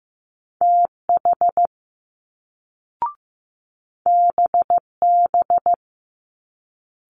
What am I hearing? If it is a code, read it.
Morse code, heard as a keyed tone, about 15 wpm, 700 Hz. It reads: TH BB
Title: Multi-letter Phonemes Quick Fire 15wpm